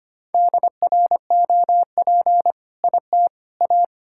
Morse code, heard as a keyed tone, about 25 wpm, 700 Hz. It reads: DROP IT A
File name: practice / 25wpm / All-Age Appropriate Jokes - 25wpm